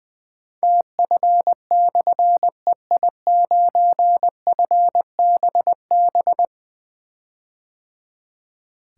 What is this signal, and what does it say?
Morse code, heard as a keyed tone, about 20 wpm, 700 Hz. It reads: TF/EI9FBB
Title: Difficult Call Signs 20wpm